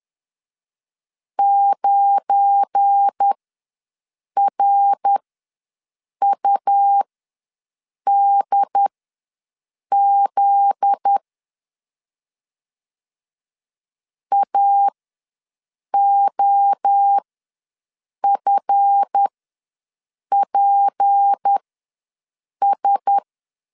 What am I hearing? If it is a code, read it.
Morse code, heard as a keyed tone, about 10 wpm, spaced as for 6 wpm, 800 Hz. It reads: 9RUDZ AOFPS
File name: LEZIONI / 7 GF9Z / 7 GF9Z ws 10wpm 3919